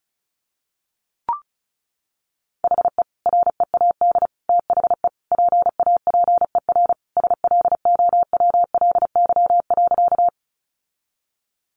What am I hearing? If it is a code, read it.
Morse code, heard as a keyed tone, about 34 wpm, 700 Hz. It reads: HE READ THE PAPER SLOWLY.
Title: Sentences from Top 1100 Words 35wpm